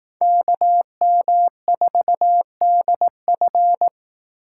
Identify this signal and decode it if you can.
Morse code, heard as a keyed tone, about 18 wpm, 700 Hz. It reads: KM4DF
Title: Speed Racing - USA 2x2 Call Signs 15wpm